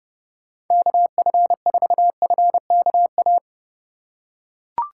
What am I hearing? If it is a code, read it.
Morse code, heard as a keyed tone, about 31 wpm, 700 Hz. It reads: KF4FKA
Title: USA Call Signs - 2x Character Spacing - 30wpm